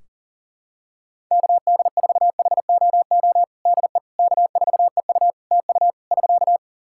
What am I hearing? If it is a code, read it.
Morse code, heard as a keyed tone, about 40 wpm, 700 Hz. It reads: KD4HOO DE K4EU TU <SK>
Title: Practice QSO - Exchange 4 - Ending - 40wpm